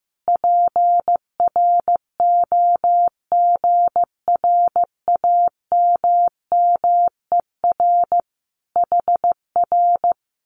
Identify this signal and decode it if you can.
Morse code, heard as a keyed tone, about 15 wpm, 700 Hz. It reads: PROGRAMMER HR
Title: Common QSO Phrases Encore 15wpm